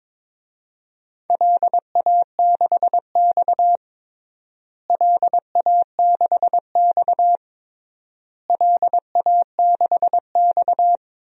Morse code, heard as a keyed tone, about 22 wpm, 700 Hz. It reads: LA6X LA6X LA6X
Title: DXCC-100 Call Signs - 4 Characters - Repeated 3x - 22wpm